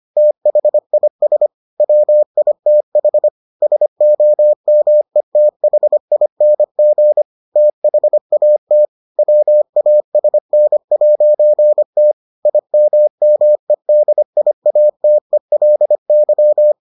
Morse code, heard as a keyed tone, about 25 wpm, 600 Hz. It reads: THIS WITH SOMETHING THAT WASN'T IMMEDIATELY